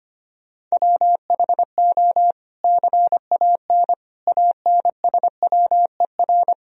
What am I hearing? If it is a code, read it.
Morse code, heard as a keyed tone, about 25 wpm, 700 Hz. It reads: WHO CAN ANSWER
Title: Sentences from Top 300 Words and Abbreviations 25wpm